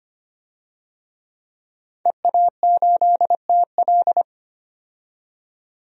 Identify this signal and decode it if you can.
Morse code, heard as a keyed tone, about 25 wpm, 700 Hz. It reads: EA8TL
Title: DXCC-100 Call Signs - 5 Characters - 25wpm